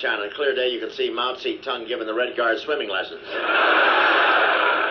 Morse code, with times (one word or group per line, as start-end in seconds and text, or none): none